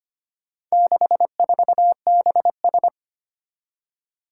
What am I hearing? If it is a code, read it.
Morse code, heard as a keyed tone, about 25 wpm, 700 Hz. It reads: 64BS
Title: Four Challenging Letter-Number Combinations 25wpm